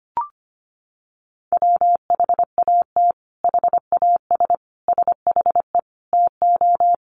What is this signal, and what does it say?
Morse code, heard as a keyed tone, about 25 wpm, 700 Hz. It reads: WHAT HAS SHE TO
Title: Sentences from Top 100 Words 25wpm